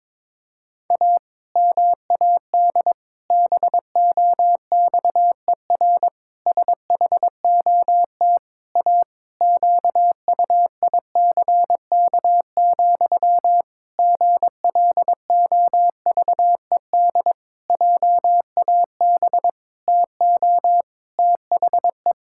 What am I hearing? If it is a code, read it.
Morse code, heard as a keyed tone, about 22 wpm, 700 Hz. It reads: A MAD BOXER SHOT A QUICK, GLOVED JAB TO THE
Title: Pangram Challenge v2 - 22wpm